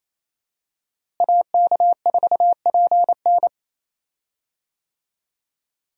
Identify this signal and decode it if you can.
Morse code, heard as a keyed tone, about 28 wpm, 700 Hz. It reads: AK4PN
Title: USA Call Signs - 5 Characters - Repeated 2x - 28wpm